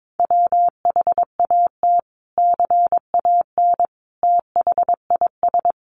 Morse code, heard as a keyed tone, about 22 wpm, 700 Hz. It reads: WHAT CAN THIS
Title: Sentences from Top 100 Words 22wpm